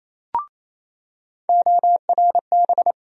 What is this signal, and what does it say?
Morse code, heard as a keyed tone, about 28 wpm, 700 Hz. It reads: ORB
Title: USA Call Sign Postfixes - 3 Letters 28wpm